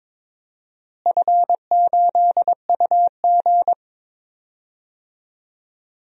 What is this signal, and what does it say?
Morse code, heard as a keyed tone, about 22 wpm, 700 Hz. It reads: F8UG
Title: DXCC-100 Call Signs - 4 Characters - Repeated 2x - 22wpm